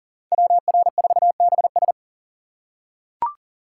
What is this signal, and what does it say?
Morse code, heard as a keyed tone, about 38 wpm, 700 Hz. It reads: WR4BS
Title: USA Call Signs - 5 Characters - 4x Character Spacing - 40wpm